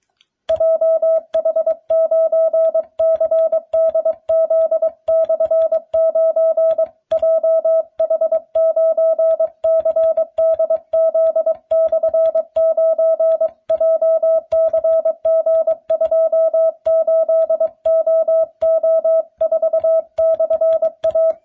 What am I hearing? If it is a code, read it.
Morse code, heard as a keyed tone, about 22 wpm, 650 Hz. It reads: JH9CDZ/9 JH9CDZ/9 JCG28OO4/A